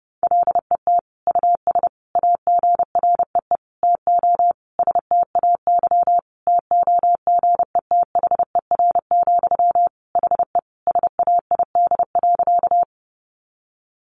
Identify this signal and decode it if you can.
Morse code, heard as a keyed tone, about 28 wpm, 700 Hz. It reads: LET US AGREE TO STAY TOGETHER, HE SAID.
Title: Sentences from Top 1100 Words 30wpm